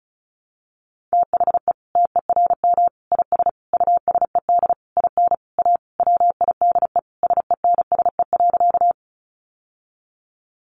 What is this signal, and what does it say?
Morse code, heard as a keyed tone, about 34 wpm, 700 Hz. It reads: THE TERM IS USED IN A WIDE SENSE.
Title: Sentences from Top 900 Words 35wpm